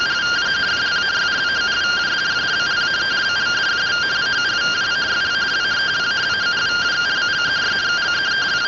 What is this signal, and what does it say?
Two-tone teleprinter signal, about 170 Hz shift.
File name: findings / 81 axial iodine 262.gen-03 rtty rx